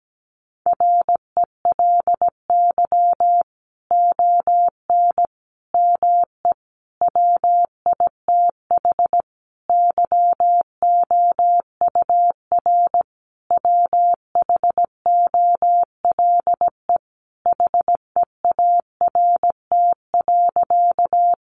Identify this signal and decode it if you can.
Morse code, heard as a keyed tone, about 17 wpm, 700 Hz. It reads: RELY ON ME WITH YOUR WHOLE HEART.
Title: Sentences from Top 1200 Words 17wpm